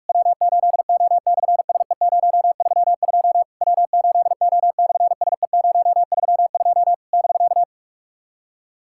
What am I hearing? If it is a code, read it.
Morse code, heard as a keyed tone, about 44 wpm, 700 Hz. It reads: W8O/SE032 W8O/SE032 <BK>
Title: SOTA QSO Practice 45wpm